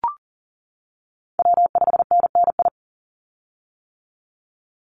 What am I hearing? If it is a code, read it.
Morse code, heard as a keyed tone, about 40 wpm, 700 Hz. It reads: W5NNI
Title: USA Call Signs 40wpm